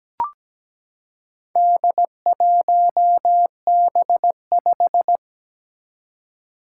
Morse code, heard as a keyed tone, about 17 wpm, 700 Hz. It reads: D1B5
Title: Four Challenging Letter-Number Combinations 17wpm